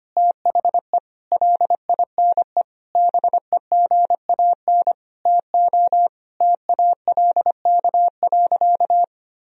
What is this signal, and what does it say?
Morse code, heard as a keyed tone, about 25 wpm, 700 Hz. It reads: THE LINE BEGAN TO TALK.
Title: Sentences from Top 1000 Words - Review Part 1 - 25wpm